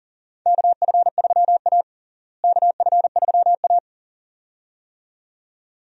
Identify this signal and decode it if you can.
Morse code, heard as a keyed tone, about 40 wpm, 700 Hz. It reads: KF3A KF3A
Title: USA Call Signs - 4 Characters - Repeated 2x - 40wpm